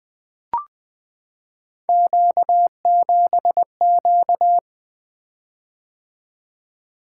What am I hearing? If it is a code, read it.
Morse code, heard as a keyed tone, about 20 wpm, 700 Hz. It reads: Q7Q
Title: Three Letter-Number Combinations - 20wpm